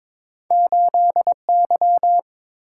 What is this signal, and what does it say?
Morse code, heard as a keyed tone, about 22 wpm, 700 Hz. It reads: 8Y
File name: practice / 22wpm / Two Letter-Number Combinations - 22wpm